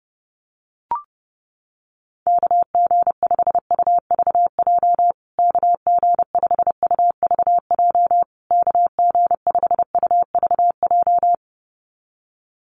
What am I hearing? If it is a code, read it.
Morse code, heard as a keyed tone, about 30 wpm, 700 Hz. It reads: KG5UVJ KG5UVJ KG5UVJ
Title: USA Call Signs Repeated 3x 30wpm